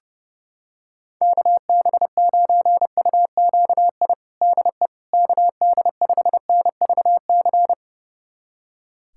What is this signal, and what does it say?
Morse code, heard as a keyed tone, about 30 wpm, 700 Hz. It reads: KB9UQI DE KD5NVC
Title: USA Callsign DE Callsign - 6 Characters - 30wpm